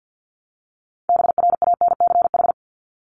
Morse code, heard as a keyed tone, about 50 wpm, 700 Hz. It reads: BRANCH
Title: Top 1751-2000 Words 50wpm